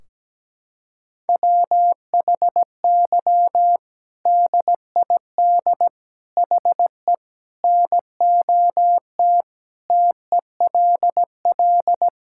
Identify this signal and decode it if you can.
Morse code, heard as a keyed tone, about 17 wpm, 700 Hz. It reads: WHY DID HE NOT TELL
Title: Sentences from Top 1600 Words 17wpm